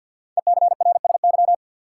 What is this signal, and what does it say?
Morse code, heard as a keyed tone, about 48 wpm, 700 Hz. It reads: ECRSY
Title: Code Groups - 5 Letters - 50wpm